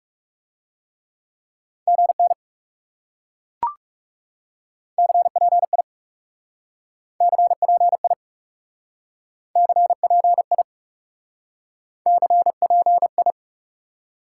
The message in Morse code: GN CPI CPI CPI CPI

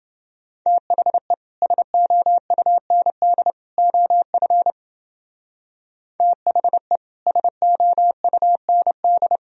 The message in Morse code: THE SOUND OF THE SOUND

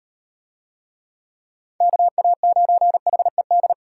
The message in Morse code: KA9HED

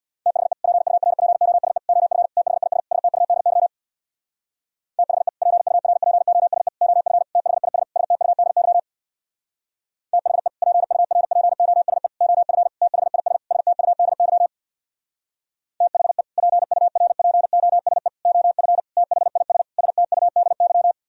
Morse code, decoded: THE PURPOSE OF THIS STUDY THE PURPOSE OF THIS STUDY THE PURPOSE OF THIS STUDY THE PURPOSE OF THIS STUDY

